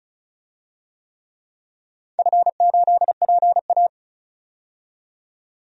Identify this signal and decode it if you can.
Morse code, heard as a keyed tone, about 36 wpm, 700 Hz. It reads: F8PA
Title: DXCC-100 Call Signs - 4 Characters - 35wpm